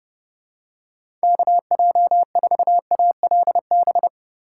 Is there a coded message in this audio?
Yes: Morse code from 1.2 to 4.1 s, heard as a keyed tone, about 31 wpm, 700 Hz. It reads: KJ4ALB